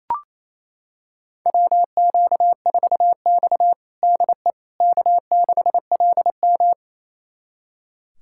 Morse code, heard as a keyed tone, about 29 wpm, 700 Hz. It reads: WQ4X DE K6LM
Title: USA Callsign DE Callsign - 4 Characters - 28wpm